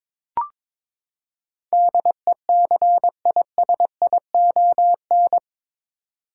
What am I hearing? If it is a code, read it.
Morse code, heard as a keyed tone, about 22 wpm, 700 Hz. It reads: DECISION
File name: practice / 22wpm / Eight Letter Words - 22wpm